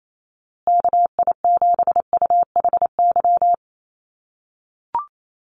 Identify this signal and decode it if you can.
Morse code, heard as a keyed tone, about 28 wpm, 700 Hz. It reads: KI7UHY
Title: USA Call Signs Repeated 2x 28wpm